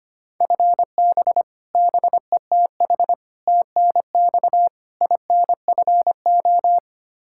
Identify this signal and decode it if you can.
Morse code, heard as a keyed tone, about 25 wpm, 700 Hz. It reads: FB BETH TNX INFO
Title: Practice QSO - Exchange 3 - More Info - 25wpm